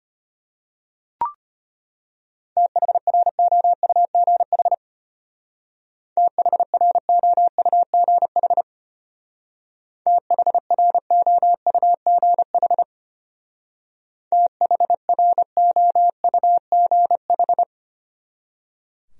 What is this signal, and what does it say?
Morse code, heard as a keyed tone, about 31 wpm, 700 Hz. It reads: THROUGH THROUGH THROUGH THROUGH
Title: Speed-Racing Top 200 Words 25wpm